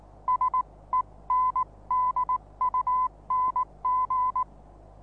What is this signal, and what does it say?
Morse code, heard as a keyed tone, about 19 wpm, spaced as for 17 wpm, 1010 Hz. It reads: SENDUNG
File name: repos / Morsekurs / DJ4UF Morsen26